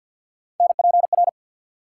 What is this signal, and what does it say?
Morse code, heard as a keyed tone, about 52 wpm, 700 Hz. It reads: NPR
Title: Sets of 5 Words - New Word Drill - 50wpm